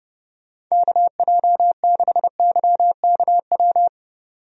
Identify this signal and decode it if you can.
Morse code, heard as a keyed tone, about 30 wpm, 700 Hz. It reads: KJ6YKW
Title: USA Call Signs - 6 Characters - 30wpm